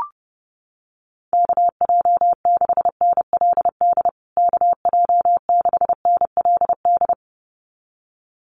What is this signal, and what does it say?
Morse code, heard as a keyed tone, about 30 wpm, 700 Hz. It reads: KJ6NLD KJ6NLD
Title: USA Call Signs Repeated 2x 30wpm